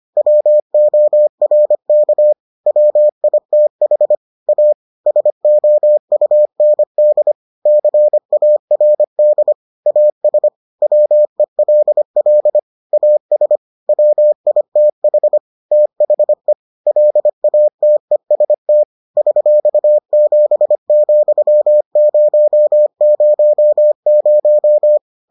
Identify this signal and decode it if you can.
Morse code, heard as a keyed tone, about 25 wpm, 600 Hz. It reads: WORK WITH A SOUND CARD AS WELL AS WITH THE LATEST $7,000